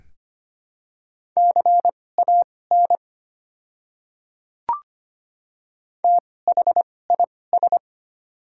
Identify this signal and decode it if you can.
Morse code, heard as a keyed tone, about 25 wpm, spaced as for 18 wpm, 700 Hz. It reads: CAN THIS
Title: 2x Character Spacing - Top 100 Words - 25wpm